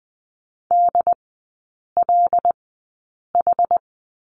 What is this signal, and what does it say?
Morse code, heard as a keyed tone, about 20 wpm, spaced as for 8 wpm, 690 Hz. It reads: DLH
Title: 159 Lesson 26e - Letter g - Three Characters 20wpm